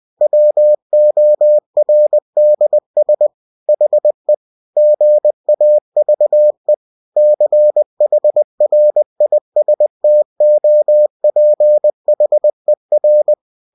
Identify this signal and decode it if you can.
Morse code, heard as a keyed tone, about 20 wpm, 600 Hz. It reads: WORDS HE GAVE CHRISTOPHER
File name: cw-books / winnie-the-pooh / winnie-20wpm-page-0098